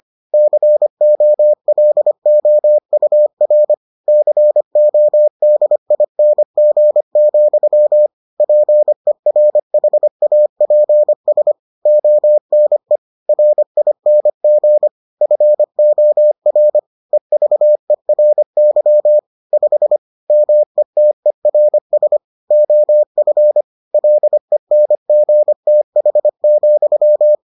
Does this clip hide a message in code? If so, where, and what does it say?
Yes: Morse code from 0.3 to 27.4 s, heard as a keyed tone, about 25 wpm, 600 Hz. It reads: COLOUR CODING, PERHAPS ONE RING FOR EVERY 5 METERS OF LENGTH,